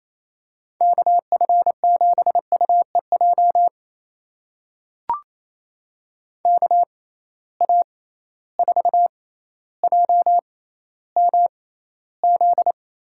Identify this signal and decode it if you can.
Morse code, heard as a keyed tone, about 28 wpm, spaced as for 14 wpm, 700 Hz. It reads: KF7UEJ KA4JMZ
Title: USA Call Signs - 6 Characters - 6x Character Spacing - 28wpm